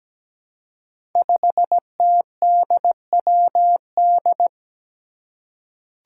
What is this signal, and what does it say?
Morse code, heard as a keyed tone, about 17 wpm, 700 Hz. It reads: 5TDWD
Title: Five Challenging Letter-Number Combinations 17wpm